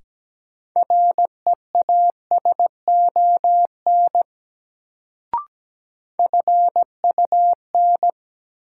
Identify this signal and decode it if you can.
Morse code, heard as a keyed tone, about 17 wpm, 700 Hz. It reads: REASON FUN
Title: Sentences from Top 700 Words - New Word Drill - 17wpm